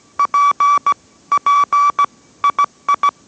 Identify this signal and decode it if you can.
Morse code, heard as a keyed tone, about 17 wpm, 1150 Hz. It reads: P P II